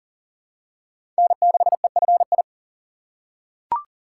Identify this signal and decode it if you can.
Morse code, heard as a keyed tone, about 40 wpm, 700 Hz. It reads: N6EFI
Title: USA Call Signs - 5 Characters - 3x Character Spacing - 40wpm